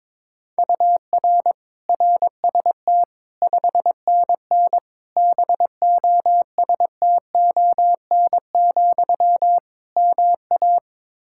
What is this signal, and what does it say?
Morse code, heard as a keyed tone, about 22 wpm, 700 Hz. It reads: UR RST 5NN BOSTON, MA
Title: Practice QSO - Exchange 1 - Beginning - 22wpm